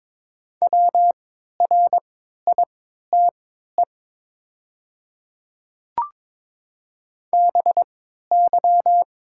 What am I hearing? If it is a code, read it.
Morse code, heard as a keyed tone, about 23 wpm, spaced as for 13 wpm, 700 Hz. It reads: WRITE BY
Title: 3x Character Spacing - Top 500 Words - 22wpm